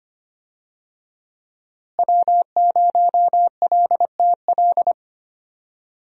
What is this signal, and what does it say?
Morse code, heard as a keyed tone, about 25 wpm, 700 Hz. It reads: W0LTL
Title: USA Call Signs - 5 Characters - Repeated 3x - 25wpm